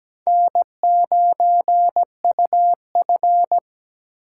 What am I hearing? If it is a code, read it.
Morse code, heard as a keyed tone, about 17 wpm, 700 Hz. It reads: N9UF
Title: USA Call Signs - 3x Character Spacing - 17wpm